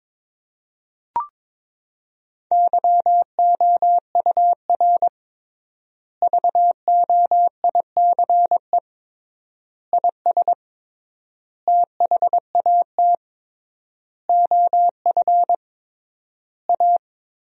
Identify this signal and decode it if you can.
Morse code, heard as a keyed tone, about 21 wpm, 700 Hz. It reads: YOUR VOICE IS THAT OF A
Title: 3x Word-Spacing - Sentences from Top 500 Words - 22wpm